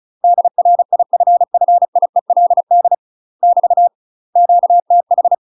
Morse code, heard as a keyed tone, about 34 wpm, 700 Hz. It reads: DRIFFIELD <BT> QTH
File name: UK MASTER QSOs TRAINING / 35WPM / UK MASTER QSOs 0014